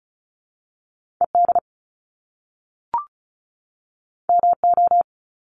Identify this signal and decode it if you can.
Morse code, heard as a keyed tone, about 36 wpm, 700 Hz. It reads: ED MO
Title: Top Two Letter Combinations in Words 35wpm